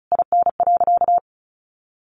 Morse code, heard as a keyed tone, about 34 wpm, 700 Hz. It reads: IN.
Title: Sentences from Top 1500 Words 35wpm